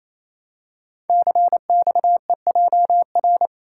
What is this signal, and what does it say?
Morse code, heard as a keyed tone, about 28 wpm, 700 Hz. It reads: CXEJR
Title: Code Groups - 5 Letter:Numbers - 28wpm